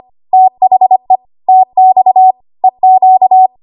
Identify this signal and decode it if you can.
Morse code, heard as a keyed tone, about 25 wpm, 750 Hz. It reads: THE TX EQ
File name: Archive / 25 WPM / 25WPM